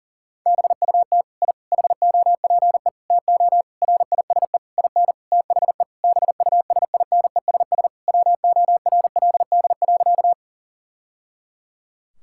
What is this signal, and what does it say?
Morse code, heard as a keyed tone, about 40 wpm, 700 Hz. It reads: BUT I HOPE TO RISE IN THE BUSINESS WORLD.
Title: Sentences from Top 1000 Words - Review Part 6 - 40wpm